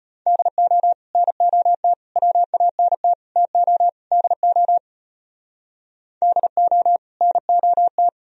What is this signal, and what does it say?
Morse code, heard as a keyed tone, about 32 wpm, 700 Hz. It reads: DO NOT WANT TO DO DO NOT